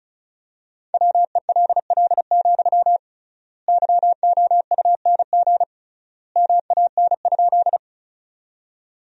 Morse code, heard as a keyed tone, about 36 wpm, 700 Hz. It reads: WELL, YOUNG MAN?
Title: 3x Word-Spacing - Sentences from Top 400 Words - 35wpm